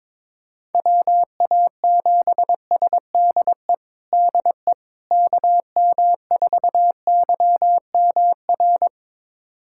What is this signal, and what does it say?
Morse code, heard as a keyed tone, about 22 wpm, 700 Hz. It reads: WA7SDE DE KM4YMR
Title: USA Callsign DE Callsign - 6 Characters - 22wpm